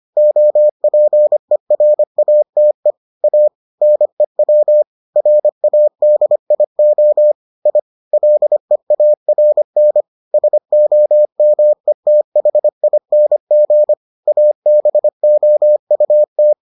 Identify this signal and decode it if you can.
Morse code, heard as a keyed tone, about 25 wpm, 600 Hz. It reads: OPERATE A NEW RADIO I LEARN SOMETHING ABOUT